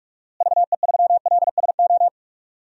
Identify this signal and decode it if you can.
Morse code, heard as a keyed tone, about 46 wpm, 700 Hz. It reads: VE3LSO
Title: DXCC-100 Call Signs - 6 Characters - Repeated 2x - 45wpm